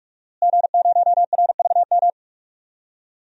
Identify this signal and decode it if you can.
Morse code, heard as a keyed tone, about 43 wpm, 700 Hz. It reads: G0RVM
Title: DXCC-100 Call Signs - 5 Characters - Repeated 3x - 45wpm